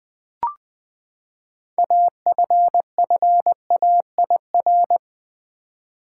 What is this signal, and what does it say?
Morse code, heard as a keyed tone, about 20 wpm, 700 Hz. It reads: AFFAIR
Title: Six Letter Words - 20wpm